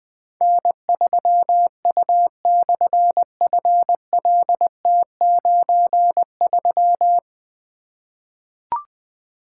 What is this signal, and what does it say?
Morse code, heard as a keyed tone, about 20 wpm, 700 Hz. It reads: N3U/FLT93
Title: Difficult Call Signs 20wpm